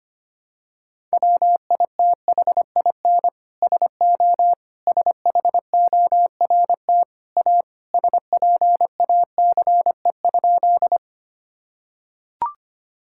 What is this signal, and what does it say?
Morse code, heard as a keyed tone, about 25 wpm, 700 Hz. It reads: WITHIN SO SHORT A SPACE?
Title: Sentences from Top 1000 Words - Review Part 5 - 25wpm